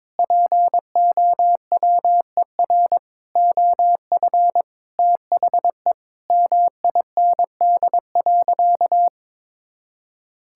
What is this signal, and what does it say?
Morse code, heard as a keyed tone, about 21 wpm, 700 Hz. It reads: POWER OF THE MIND.
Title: Sentences from Top 1000 Words - Review Part 5 - 22wpm